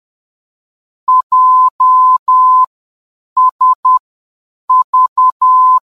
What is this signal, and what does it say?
Morse code, heard as a keyed tone, about 10 wpm, spaced as for 7 wpm, 1020 Hz. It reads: JSV
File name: JSV-VOR DME